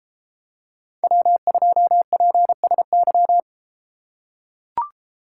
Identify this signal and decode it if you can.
Morse code, heard as a keyed tone, about 31 wpm, 700 Hz. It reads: W2PSY